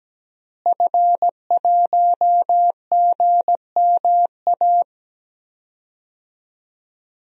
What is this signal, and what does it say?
Morse code, heard as a keyed tone, about 17 wpm, 700 Hz. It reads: F1GMA
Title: DXCC-100 Call Signs - 5 Characters - Repeated 3x - 17wpm